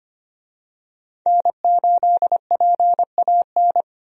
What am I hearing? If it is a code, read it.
Morse code, heard as a keyed tone, about 25 wpm, 690 Hz. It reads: N8PAN
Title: USA Call Signs - 5 Characters - 25wpm